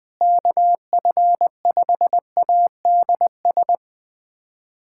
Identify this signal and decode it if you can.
Morse code, heard as a keyed tone, about 20 wpm, 700 Hz. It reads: KF5ADS
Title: USA Callsign DE Callsign 20wpm